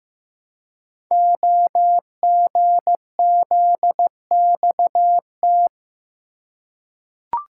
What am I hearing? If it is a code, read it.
Morse code, heard as a keyed tone, about 15 wpm, 700 Hz. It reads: OGZXT
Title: Code Groups - 5 Letters - 15wpm